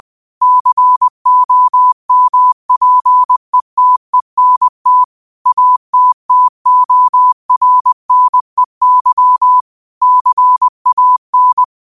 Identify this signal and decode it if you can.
Morse code, heard as a keyed tone, about 20 wpm, 1000 Hz. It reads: COMPETENT ATTORNEY CAN
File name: qotdfiles / 20wpm